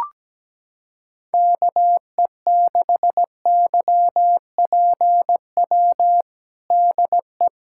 Morse code, heard as a keyed tone, about 17 wpm, 700 Hz. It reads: KE6YPW DE
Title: USA Callsign DE Callsign - 6 Characters - 17wpm